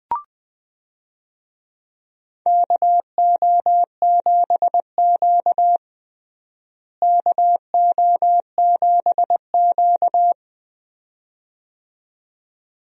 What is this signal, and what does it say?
Morse code, heard as a keyed tone, about 21 wpm, 700 Hz. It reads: KO7Q KO7Q
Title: USA Call Signs - 4 Characters - Repeated 2x - 20wpm